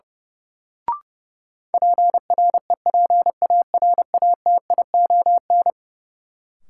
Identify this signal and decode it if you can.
Morse code, heard as a keyed tone, about 32 wpm, 700 Hz. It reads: PREPARATION
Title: Sentences from Top 1400 Words - New Word Drill Alt - 30wpm